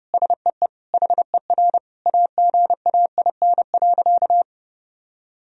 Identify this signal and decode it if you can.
Morse code, heard as a keyed tone, about 30 wpm, 700 Hz. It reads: SEE HER AGAIN.
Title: Sentences from Top 1000 Words - Review Part 4 - 30wpm